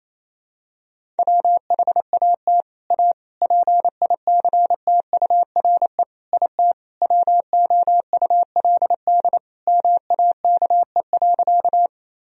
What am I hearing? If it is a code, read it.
Morse code, heard as a keyed tone, about 28 wpm, 700 Hz. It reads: WHAT A PICTURE IT WOULD MAKE.
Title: Sentences from Top 1000 Words - Review Part 1 - 28wpm